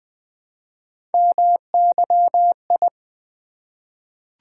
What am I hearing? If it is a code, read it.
Morse code, heard as a keyed tone, about 20 wpm, 700 Hz. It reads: MYI
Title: Three Letter-Number Combinations - 20wpm